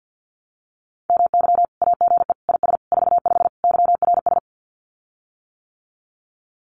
Morse code, heard as a keyed tone, about 50 wpm, 700 Hz. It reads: MY AGE IS 45 YRS
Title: Common QSO Phrases Encore 50wpm